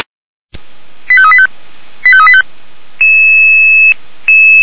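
Five-tone selective call (ZVEI), heard as quick runs of five tones, tones of about 70 ms. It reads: 85385 85385
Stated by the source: ZVEI five-tone selective call received by SDR.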